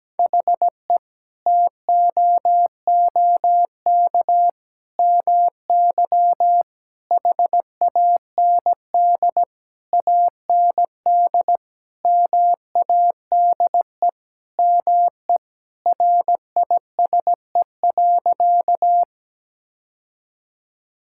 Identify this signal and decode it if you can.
Morse code, heard as a keyed tone, about 17 wpm, 700 Hz. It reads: HE TOOK MY HAND AND MADE ME RISE.
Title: Sentences from Top 1000 Words - Review Part 6 - 17wpm